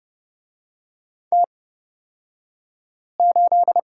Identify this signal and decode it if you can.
Morse code, heard as a keyed tone, about 30 wpm, spaced as for 5 wpm, 700 Hz. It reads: T8
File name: Lesson 38d - Letter q - Two Characters 30wpm